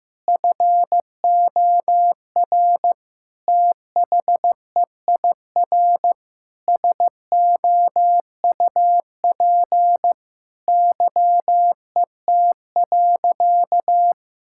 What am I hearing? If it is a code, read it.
Morse code, heard as a keyed tone, about 15 wpm, 700 Hz. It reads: FOR THEIR SOUP YET.
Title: Sentences from Top 1000 Words - Review Part 6 - 15wpm